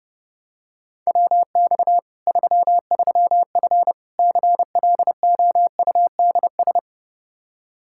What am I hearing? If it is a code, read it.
Morse code, heard as a keyed tone, about 32 wpm, 700 Hz. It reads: WX 33F CLOUDS
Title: Practice QSO - Exchange 2 - Info - 30wpm